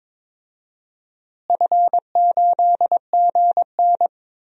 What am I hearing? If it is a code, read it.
Morse code, heard as a keyed tone, about 22 wpm, 700 Hz. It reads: F8GN
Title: DXCC-100 Call Signs - 4 Characters - Repeated 3x - 22wpm